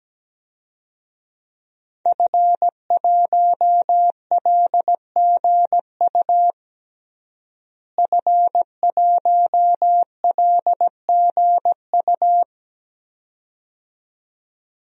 Morse code, heard as a keyed tone, about 17 wpm, 700 Hz. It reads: F1LGU F1LGU
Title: DXCC-100 Call Signs - 5 Characters - Repeated 2x - 17wpm